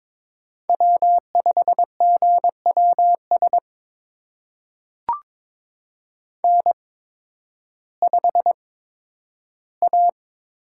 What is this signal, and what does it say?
Morse code, heard as a keyed tone, about 23 wpm, spaced as for 12 wpm, 700 Hz. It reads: W5GWS N5A
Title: USA Call Signs - 5 Characters - 8x Character Spacing - 22wpm